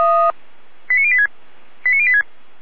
Five-tone selective call (ZVEI), heard as quick runs of five tones, tones of about 70 ms. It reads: 89086 89086